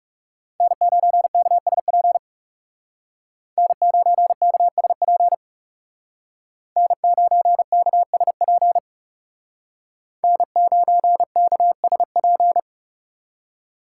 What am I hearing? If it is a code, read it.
Morse code, heard as a keyed tone, about 38 wpm, 700 Hz. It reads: N9KSP N9KSP N9KSP N9KSP